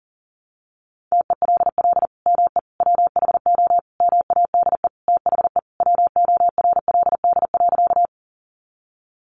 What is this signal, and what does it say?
Morse code, heard as a keyed tone, about 40 wpm, 700 Hz. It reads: TELL ME WHO MADE THE WORLD.
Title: Sentences from Top 200 Words 40wpm